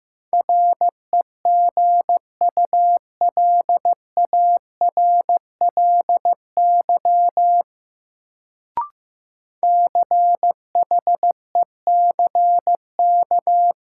Morse code, heard as a keyed tone, about 15 wpm, 700 Hz. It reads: REGULARLY CHECK
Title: Sentences from Top 1400 Words - New Word Drill Alt - 15wpm